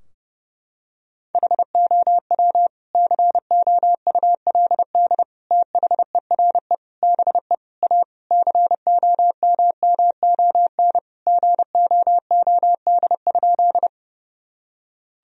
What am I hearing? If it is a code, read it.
Morse code, heard as a keyed tone, about 32 wpm, 700 Hz. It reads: HOW COULD THERE BE A COMMON GOOD?
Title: Sentences from Top 1000 Words - Review Part 3 - 30wpm